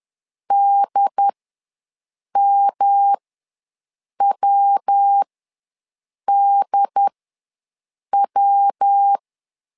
Morse code, heard as a keyed tone, about 10 wpm, spaced as for 6 wpm, 800 Hz. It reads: DMWDW
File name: LEZIONI / 3 MWRO / 3 MW ws 10wpm 2213